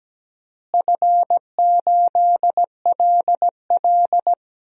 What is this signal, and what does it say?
Morse code, heard as a keyed tone, about 17 wpm, 700 Hz. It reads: F8LL
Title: DXCC-100 Call Signs - 4 Characters - Repeated 3x - 17wpm